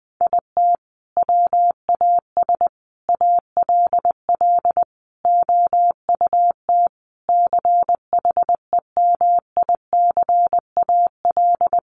Morse code, heard as a keyed tone, about 20 wpm, 700 Hz. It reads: IT WAS ALL OUT CHEMICAL
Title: Sentences from Top 1200 Words 20wpm